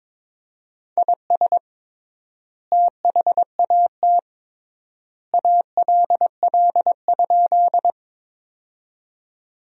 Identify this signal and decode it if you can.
Morse code, heard as a keyed tone, about 22 wpm, 700 Hz. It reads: IS THAT ALL?
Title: 3x Word-Spacing - Sentences from Top 100 Words - 22wpm